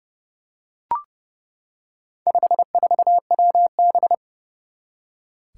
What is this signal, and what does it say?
Morse code, heard as a keyed tone, about 30 wpm, 700 Hz. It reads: 54WB
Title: Four Challenging Letter-Number Combinations 30wpm